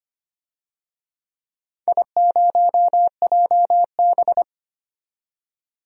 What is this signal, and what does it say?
Morse code, heard as a keyed tone, about 25 wpm, 700 Hz. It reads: I0JB
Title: DXCC-100 Call Signs - 4 Characters - 25wpm